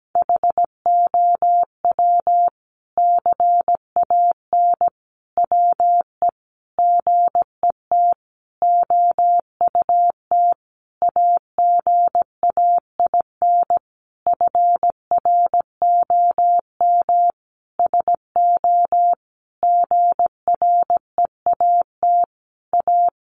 Morse code, heard as a keyed tone, about 17 wpm, 700 Hz. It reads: HOW CAN WE GET OUT AGAIN FROM SO GREAT A
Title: Sentences from Top 1200 Words 17wpm